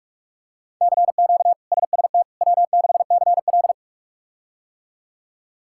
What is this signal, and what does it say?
Morse code, heard as a keyed tone, about 46 wpm, 700 Hz. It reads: CQ SST W6CL
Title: K1USN SST QSO Practice - 45wpm